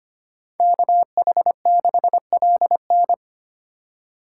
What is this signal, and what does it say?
Morse code, heard as a keyed tone, about 31 wpm, 700 Hz. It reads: KH6LN
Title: Speed Racing - USA 2x2 Call Signs 25wpm